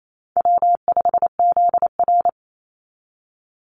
Morse code, heard as a keyed tone, about 28 wpm, 700 Hz. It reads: W5ZR
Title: Easy Call Signs Repeated 3x - 28wpm